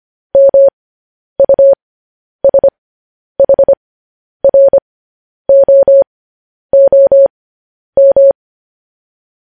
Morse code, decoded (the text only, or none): MUSHROOM